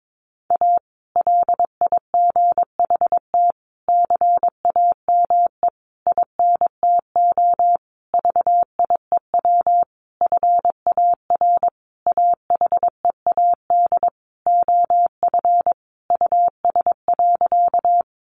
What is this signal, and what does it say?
Morse code, heard as a keyed tone, about 22 wpm, 700 Hz. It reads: A LIGHT CAME INTO VIEW FAR AHEAD OF US.
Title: Sentences from Top 1100 Words 22wpm